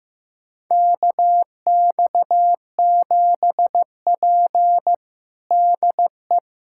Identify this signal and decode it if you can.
Morse code, heard as a keyed tone, about 15 wpm, 700 Hz. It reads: KX7P DE
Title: USA Callsign DE Callsign - 4 Characters - 15wpm